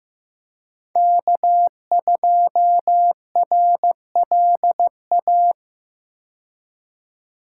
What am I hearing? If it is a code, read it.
Morse code, heard as a keyed tone, about 15 wpm, 700 Hz. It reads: K2RLA
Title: Code Groups - 5 Letter:Numbers - 15wpm